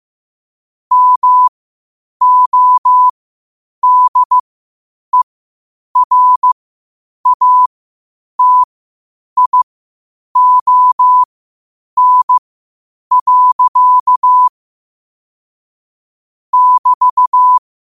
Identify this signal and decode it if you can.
Morse code, heard as a keyed tone, about 15 wpm, spaced as for 8 wpm, 1000 Hz. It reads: MODERATION. <BT>
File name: qotdfiles / 8wpm